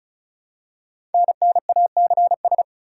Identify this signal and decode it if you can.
Morse code, heard as a keyed tone, about 36 wpm, 700 Hz. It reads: NNACS
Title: Code Groups - 5 Letters - 35wpm